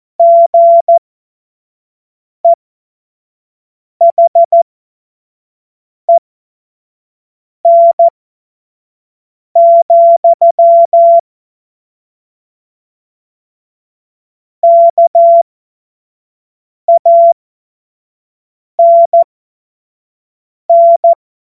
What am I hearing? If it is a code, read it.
Morse code, heard as a keyed tone, about 14 wpm, spaced as for 5 wpm, 680 Hz. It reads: GEHEN, KANN
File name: Witz-09-September-2025-25-F70